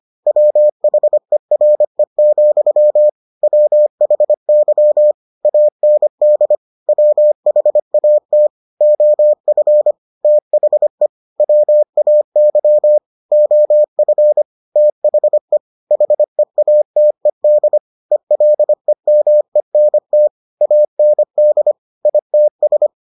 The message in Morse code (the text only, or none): WHERE, WHY AND WHAT OF THE WAY OF THE HEATED ELEMENT AND ITS